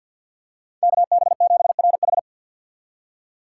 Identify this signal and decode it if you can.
Morse code, heard as a keyed tone, about 53 wpm, 690 Hz. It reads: KB7RH
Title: USA Call Signs - 6x Character Spacing - 50wpm